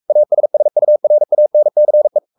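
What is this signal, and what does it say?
Morse code, heard as a keyed tone, about 44 wpm, 600 Hz. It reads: ASSURANCE
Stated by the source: Morse code at 43 wpm sending Ken Follett's novel Never in English, from CW med Gnister.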